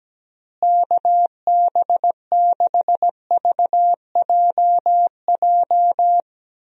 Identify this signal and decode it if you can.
Morse code, heard as a keyed tone, about 17 wpm, 700 Hz. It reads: KB6VJJ
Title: USA Call Signs - 6 Characters - Repeated 3x - 17wpm